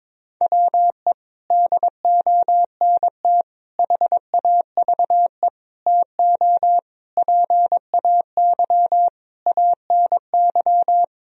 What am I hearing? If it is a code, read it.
Morse code, heard as a keyed tone, about 22 wpm, 700 Hz. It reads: WE DONT HAVE TO PAY ANY
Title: Sentences from Top 1000 Words - Review Part 2 - 22wpm